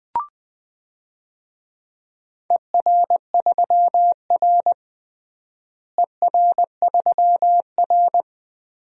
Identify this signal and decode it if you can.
Morse code, heard as a keyed tone, about 20 wpm, 700 Hz. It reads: ER3R ER3R
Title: DXCC-100 Call Signs - 4 Characters - Repeated 3x - 20wpm